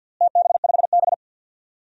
Morse code, heard as a keyed tone, about 52 wpm, 700 Hz. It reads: TB5B
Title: DXCC-100 Call Signs - 4 Characters - Repeated 2x - 50wpm